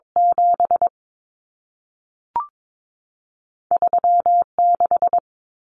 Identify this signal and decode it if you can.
Morse code, heard as a keyed tone, about 22 wpm, spaced as for 3 wpm, 700 Hz. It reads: 73 6
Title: Common Cut Numbers - 2-Digit 22wpm